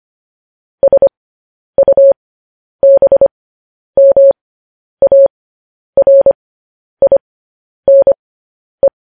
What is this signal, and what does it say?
Morse code, heard as a keyed tone, about 25 wpm, spaced as for 10 wpm, 550 Hz. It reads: SUBMARINE
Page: news-headlines-in-morse-code-at-10-wpm